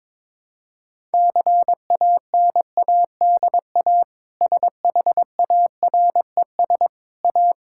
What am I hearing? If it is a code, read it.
Morse code, heard as a keyed tone, about 22 wpm, 700 Hz. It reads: CANADA SHARES A
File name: practice / 22wpm / Sentences with 4-Word Phrases - 22wpm